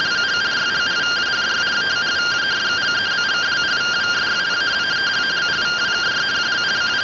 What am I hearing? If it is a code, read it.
Two-tone teleprinter signal, about 170 Hz shift.
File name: findings / CT AXIAL -142.gen-12 rtty rx